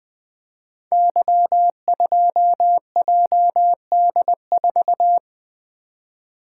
Morse code, heard as a keyed tone, about 20 wpm, 700 Hz. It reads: Y2JD4
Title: Code Groups - 5 Letter:Numbers - 20wpm